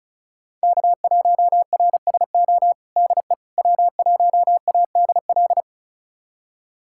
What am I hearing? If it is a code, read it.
Morse code, heard as a keyed tone, about 36 wpm, 690 Hz. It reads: K1RSO DE W1ADL
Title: USA Callsign DE Callsign - 5 Characters - 35wpm